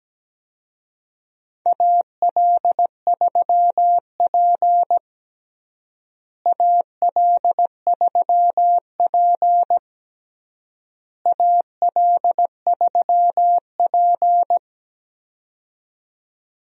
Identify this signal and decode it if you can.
Morse code, heard as a keyed tone, about 17 wpm, 700 Hz. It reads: AL3P AL3P AL3P
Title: USA Call Signs - 4 Characters - Repeated 3x - 17wpm